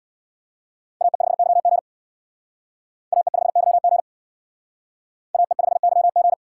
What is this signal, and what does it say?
Morse code, heard as a keyed tone, about 63 wpm, 700 Hz. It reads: AE5YG AE5YG AE5YG